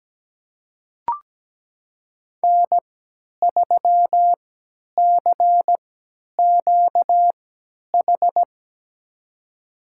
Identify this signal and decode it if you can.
Morse code, heard as a keyed tone, about 17 wpm, spaced as for 10 wpm, 700 Hz. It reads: N3CQH
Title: USA Call Signs - 5 Characters - 3x Character Spacing - 17wpm